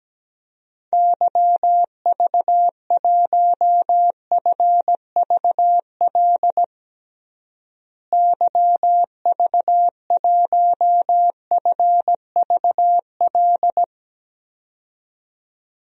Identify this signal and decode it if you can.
Morse code, heard as a keyed tone, about 17 wpm, 700 Hz. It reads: YV1FVL YV1FVL
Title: DXCC-100 Call Signs - 6 Characters - Repeated 2x - 17wpm